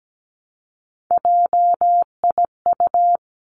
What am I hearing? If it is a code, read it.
Morse code, heard as a keyed tone, about 17 wpm, 700 Hz. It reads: JIU
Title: Three Challenging Letter-Number Combinations 17wpm